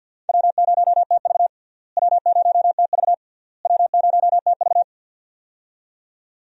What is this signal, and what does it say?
Morse code, heard as a keyed tone, about 46 wpm, 700 Hz. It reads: W0TV W0TV W0TV
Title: USA Call Signs - 4 Characters - Repeated 3x - 50wpm